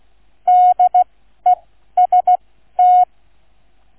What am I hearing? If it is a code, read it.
Morse code, heard as a keyed tone, about 16 wpm, spaced as for 12 wpm, 730 Hz. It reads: DEST
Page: QRS International Flight Contest 2025 Demo Flight NR 00 Sound files Listen-to or download incoming QNC sound file